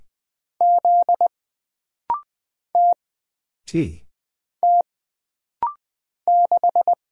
Morse code, heard as a keyed tone, about 20 wpm, 700 Hz. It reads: Z T T 6
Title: Single Character (Letter:Number:Punctuation:Prosign) - Rapid-Fire - 20wpm